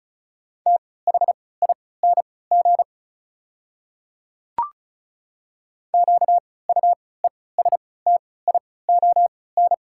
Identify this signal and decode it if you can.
Morse code, heard as a keyed tone, about 34 wpm, spaced as for 20 wpm, 690 Hz. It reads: THING QUESTION
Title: 3x Character Spacing - Top 500 Words - 35wpm